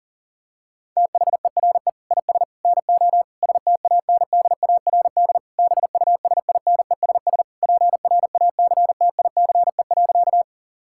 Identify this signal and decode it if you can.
Morse code, heard as a keyed tone, about 40 wpm, 700 Hz. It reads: THERE IS NO STANDARD BUSINESS PRACTICE.
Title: Sentences from Top 1000 Words - Review Part 6 - 40wpm